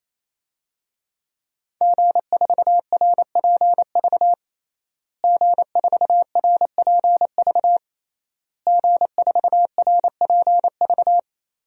Morse code, heard as a keyed tone, about 28 wpm, 700 Hz. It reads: G4RPV G4RPV G4RPV
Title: DXCC-100 Call Signs - 5 Characters - Repeated 3x - 28wpm